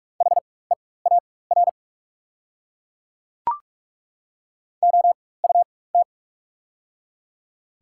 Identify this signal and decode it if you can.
Morse code, heard as a keyed tone, about 43 wpm, spaced as for 21 wpm, 690 Hz. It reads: HEAR OUT
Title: 4x Character Spacing - Top 300 Words - 45wpm